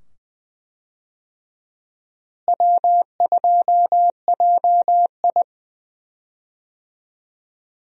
Morse code, heard as a keyed tone, about 20 wpm, 700 Hz. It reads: W2JI
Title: USA Call Signs - 4 Characters - Repeated 2x - 20wpm